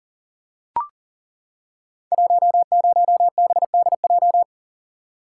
Morse code, heard as a keyed tone, about 41 wpm, 690 Hz. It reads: 10BDJ
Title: Code Groups - 5 Letter:Numbers - 40wpm